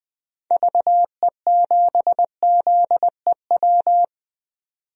Morse code, heard as a keyed tone, about 20 wpm, 700 Hz. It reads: VE7ZEW
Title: DXCC-100 Call Signs - 6 Characters - 20wpm